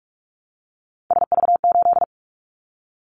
Morse code, heard as a keyed tone, about 46 wpm, 700 Hz. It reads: SV8
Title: Callsign Prefixes for Top 50 DXCC - 45wpm